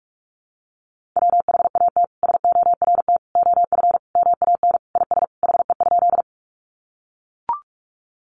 Morse code, heard as a keyed tone, about 44 wpm, 700 Hz. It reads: WHAT SORT OF MAN IS HE?
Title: Sentences from Top 1300 Words 45wpm